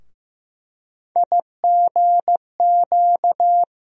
Morse code, heard as a keyed tone, about 15 wpm, 700 Hz. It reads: IGQ
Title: USA Call Sign Postfixes - 3 Letters 15wpm